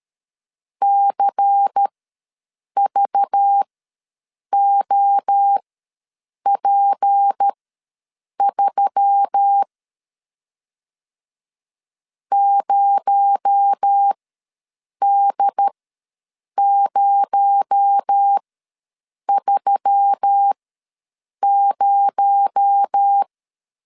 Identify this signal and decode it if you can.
Morse code, heard as a keyed tone, about 13 wpm, spaced as for 7 wpm, 800 Hz. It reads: CVOP3 0D030
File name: LEZIONI / 6 PKX0 / 6 PKX0 ws 12wpm 3448